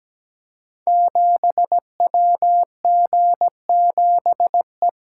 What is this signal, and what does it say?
Morse code, heard as a keyed tone, about 17 wpm, 700 Hz. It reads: 7WG7E
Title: Code Groups - 5 Letter:Numbers - 17wpm